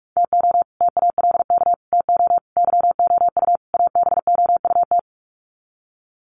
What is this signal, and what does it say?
Morse code, heard as a keyed tone, about 44 wpm, 700 Hz. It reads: TO TALK TO YOU ABOUT
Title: Sets of 5 Words Encore 45wpm